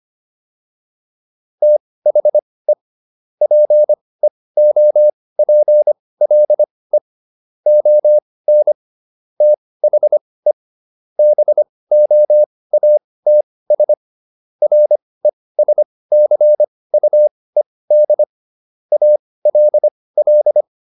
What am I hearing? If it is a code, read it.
Morse code, heard as a keyed tone, about 25 wpm, spaced as for 18 wpm, 600 Hz. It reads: THE PEOPLE ON THE BOATS RESCUED ALL